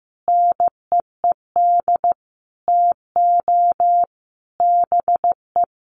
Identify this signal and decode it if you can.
Morse code, heard as a keyed tone, about 15 wpm, 700 Hz. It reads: NEED TO BE
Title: Sets of 5 Words Encore 15wpm